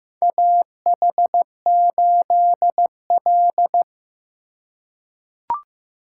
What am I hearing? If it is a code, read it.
Morse code, heard as a keyed tone, about 15 wpm, 700 Hz. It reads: AH8L
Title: USA Call Signs - 4 Characters - 4x Character Spacing - 15wpm